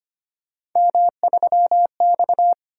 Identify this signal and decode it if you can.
Morse code, heard as a keyed tone, about 25 wpm, 700 Hz. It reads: M3X
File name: Three Letter-Number Combinations - 25wpm